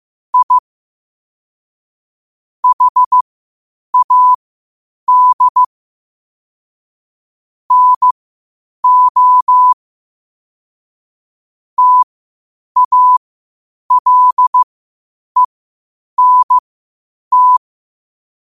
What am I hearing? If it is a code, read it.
Morse code, heard as a keyed tone, about 15 wpm, spaced as for 8 wpm, 1000 Hz. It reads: I HAD NO TALENT
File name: qotdfiles / 8wpm